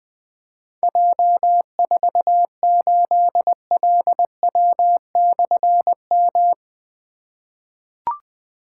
Morse code, heard as a keyed tone, about 20 wpm, 700 Hz. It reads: J48LW/M
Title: Difficult Call Signs 20wpm